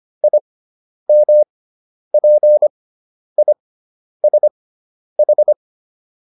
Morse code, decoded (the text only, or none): IMPISH